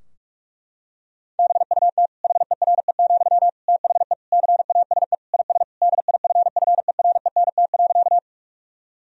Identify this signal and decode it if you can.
Morse code, heard as a keyed tone, about 46 wpm, 700 Hz. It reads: BUT HERE, THE CASE IS DIFFERENT.
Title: Sentences from Top 1000 Words - Review Part 3 - 45wpm